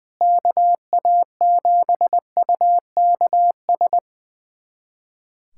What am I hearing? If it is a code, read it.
Morse code, heard as a keyed tone, about 20 wpm, 700 Hz. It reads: KA7UKS
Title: USA Callsign DE Callsign - 6 Characters - 20wpm